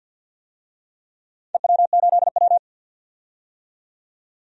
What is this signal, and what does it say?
Morse code, heard as a keyed tone, about 47 wpm, 700 Hz. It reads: EW8W
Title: DXCC-100 Call Signs - 4 Characters - 50wpm